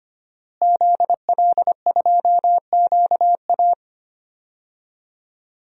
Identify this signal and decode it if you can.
Morse code, heard as a keyed tone, about 25 wpm, 700 Hz. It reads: ZL2QA
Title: DXCC-100 Call Signs - 5 Characters - Repeated 2x - 25wpm